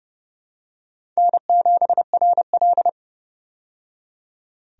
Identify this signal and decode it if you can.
Morse code, heard as a keyed tone, about 30 wpm, 700 Hz. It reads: N7RL
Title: USA Call Signs - 4 Characters - 30wpm